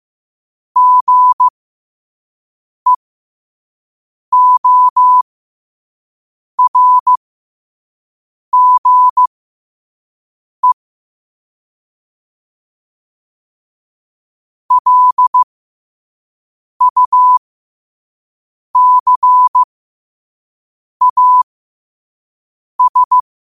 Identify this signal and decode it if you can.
Morse code, heard as a keyed tone, about 15 wpm, spaced as for 5 wpm, 1000 Hz. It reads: GEORGE LUCAS
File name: qotdfiles / 5wpm